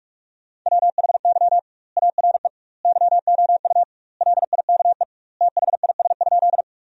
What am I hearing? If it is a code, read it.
Morse code, heard as a keyed tone, about 41 wpm, 700 Hz. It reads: WHY ARE YOU LIKE THIS?
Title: 2x Word-Spacing - Sentences from Top 200 Words - 45wpm